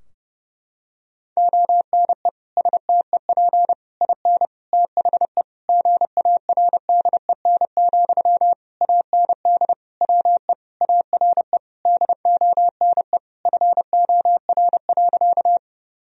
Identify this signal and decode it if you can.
Morse code, heard as a keyed tone, about 30 wpm, 700 Hz. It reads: ONE STEP IN THE GARDEN, AND WE ARE DONE FOR.
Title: Sentences from Top 1000 Words - Review Part 4 - 30wpm